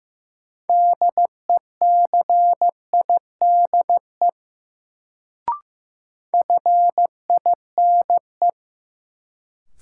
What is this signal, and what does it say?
Morse code, heard as a keyed tone, about 15 wpm, 700 Hz. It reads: DECIDE FINE
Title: Sentences from Top 500 Words - New Word Drill - 15wpm